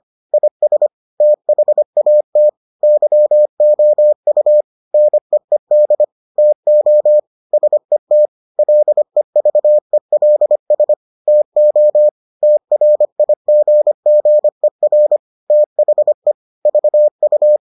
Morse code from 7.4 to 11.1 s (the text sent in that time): SET LEVELS